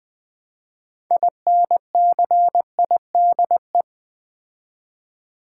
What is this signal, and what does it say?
Morse code, heard as a keyed tone, about 20 wpm, 700 Hz. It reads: INCIDE
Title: Build A Word - 3 Letters At A Time - 9-Letter Words - 20wpm